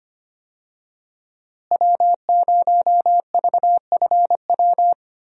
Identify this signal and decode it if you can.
Morse code, heard as a keyed tone, about 25 wpm, 700 Hz. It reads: W0VFW
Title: USA Call Signs - 5 Characters - Repeated 2x - 25wpm